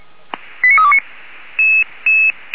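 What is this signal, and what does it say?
Five-tone selective call (ZVEI), heard as quick runs of five tones, tones of about 70 ms. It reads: 89329
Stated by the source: ZVEI five-tone selective call received via SDR.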